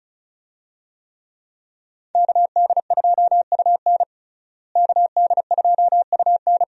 Morse code, heard as a keyed tone, about 36 wpm, 700 Hz. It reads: KD2UN KD2UN
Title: USA Call Signs - 5 Characters - Repeated 3x - 35wpm